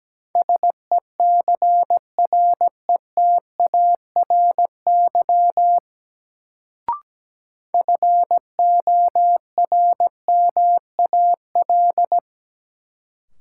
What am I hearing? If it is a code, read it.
Morse code, heard as a keyed tone, about 17 wpm, 700 Hz. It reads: SECRETARY FORMAL
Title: Sentences from Top 1300 Words - New Word Drill - 17wpm